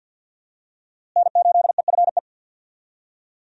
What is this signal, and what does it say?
Morse code, heard as a keyed tone, about 49 wpm, 700 Hz. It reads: N8EFE
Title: USA Call Signs - 5 Characters - Repeated 3x - 50wpm